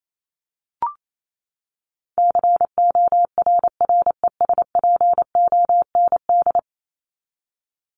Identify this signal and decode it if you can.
Morse code, heard as a keyed tone, about 28 wpm, 700 Hz. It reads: CORRESPOND
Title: ING Suffix - Top 100 Words with Base Words 28wpm